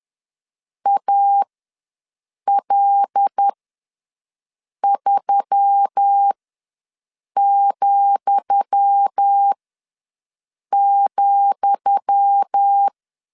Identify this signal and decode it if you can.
Morse code, heard as a keyed tone, about 10 wpm, spaced as for 5 wpm, 800 Hz. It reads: AL3,,
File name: LEZIONI / 10 SIGNS / 10D COMA ws 10wpm 4314